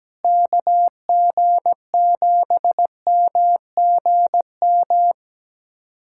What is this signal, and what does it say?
Morse code, heard as a keyed tone, about 17 wpm, 700 Hz. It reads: KG7MGM
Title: USA Callsign DE Callsign 17wpm